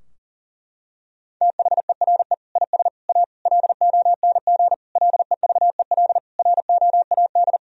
Morse code, heard as a keyed tone, about 40 wpm, 700 Hz. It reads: THERE IS A LONG LEVEL ROAD
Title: Sentences from Top 1000 Words - Review Part 4 - 40wpm